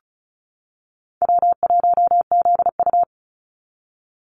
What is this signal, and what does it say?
Morse code, heard as a keyed tone, about 34 wpm, 700 Hz. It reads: W1ZU
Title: NA SOTA Activator Callsigns 35wpm